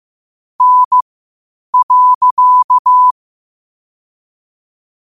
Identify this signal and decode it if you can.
Morse code, heard as a keyed tone, about 15 wpm, spaced as for 8 wpm, 1000 Hz. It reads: N.
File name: qotdfiles / 8wpm